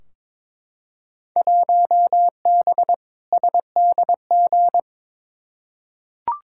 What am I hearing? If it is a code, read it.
Morse code, heard as a keyed tone, about 22 wpm, 700 Hz. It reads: 1B SDG
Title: Field Day Exchanges Encore - 22wpm